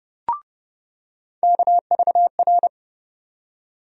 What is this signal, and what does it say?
Morse code, heard as a keyed tone, about 30 wpm, 700 Hz. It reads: KVR
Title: USA Call Sign Postfixes - 3 Letters 30wpm